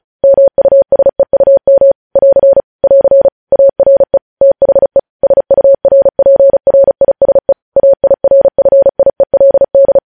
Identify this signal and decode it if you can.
Morse code, heard as a keyed tone, about 36 wpm, 550 Hz. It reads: MUSEUM <AR> <AR> ARE THE SURPRISE AIRFIELD
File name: podcast / News headlines-1x-35x35